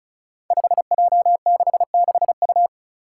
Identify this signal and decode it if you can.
Morse code, heard as a keyed tone, about 36 wpm, 700 Hz. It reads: 5J66U
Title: Five Challenging Letter-Number Combinations 35wpm